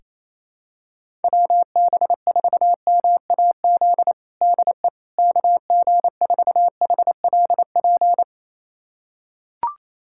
Morse code, heard as a keyed tone, about 28 wpm, 700 Hz. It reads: WB4MAZ DE KG4HLP
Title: USA Callsign DE Callsign - 6 Characters - 28wpm